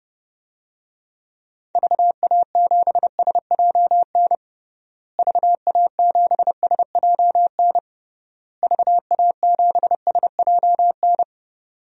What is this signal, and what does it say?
Morse code, heard as a keyed tone, about 32 wpm, 700 Hz. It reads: VA7SJN VA7SJN VA7SJN
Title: DXCC-100 Call Signs - 6 Characters - Repeated 3x - 30wpm